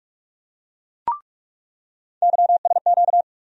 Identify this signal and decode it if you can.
Morse code, heard as a keyed tone, about 46 wpm, 700 Hz. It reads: YSQ
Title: USA Call Sign Postfixes - 3 Letters 45wpm